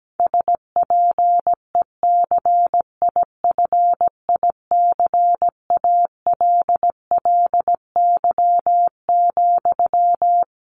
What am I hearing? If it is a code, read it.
Morse code, heard as a keyed tone, about 17 wpm, 700 Hz. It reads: SPECIFICALLY,
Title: Sentences from Top 1100 Words 17wpm